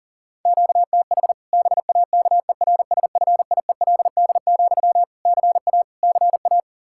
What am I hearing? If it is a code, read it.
Morse code, heard as a keyed tone, about 40 wpm, 700 Hz. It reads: QTH BAKERSFIELD, CA CA
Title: Practice QSO - Exchange 1 - Beginning - 40wpm